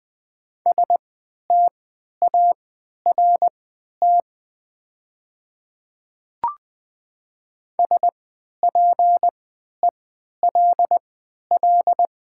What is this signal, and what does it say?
Morse code, heard as a keyed tone, about 20 wpm, spaced as for 11 wpm, 690 Hz. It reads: START SPELL
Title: 3x Character Spacing - Top 300 Words - 20wpm